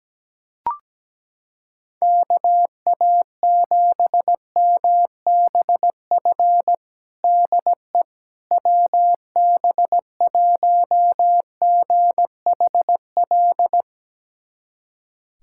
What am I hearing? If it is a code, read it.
Morse code, heard as a keyed tone, about 17 wpm, 700 Hz. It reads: KA7MBF DE WB1GHL
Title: USA Callsign DE Callsign - 6 Characters - 17wpm